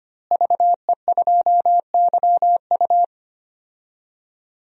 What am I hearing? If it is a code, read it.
Morse code, heard as a keyed tone, about 25 wpm, 700 Hz. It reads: VE2YU
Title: DXCC-100 Call Signs - 5 Characters - Repeated 2x - 25wpm